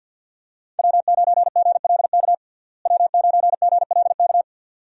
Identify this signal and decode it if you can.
Morse code, heard as a keyed tone, about 47 wpm, 700 Hz. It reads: W9GLK W9GLK
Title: USA Call Signs - 5 Characters - Repeated 2x - 50wpm